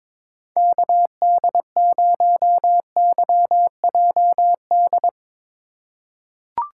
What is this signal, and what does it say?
Morse code, heard as a keyed tone, about 22 wpm, 700 Hz. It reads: KD0YJD
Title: USA Call Signs - 2x Character Spacing - 22wpm